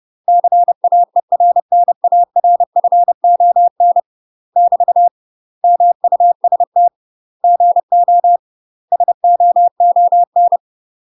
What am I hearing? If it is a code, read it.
Morse code, heard as a keyed tone, about 30 wpm, 700 Hz. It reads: CAERNARFON <BT> MUST GO SOON
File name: UK MASTER QSOs TRAINING / 30WPM / UK MASTER QSOs 0045